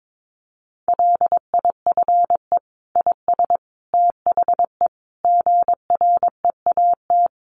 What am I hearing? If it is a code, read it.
Morse code, heard as a keyed tone, about 22 wpm, 700 Hz. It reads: LIFE IS THE GREAT
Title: Sentences from Top 900 Words 22wpm